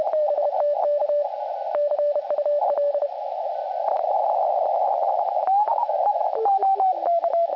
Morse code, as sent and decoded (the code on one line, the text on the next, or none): none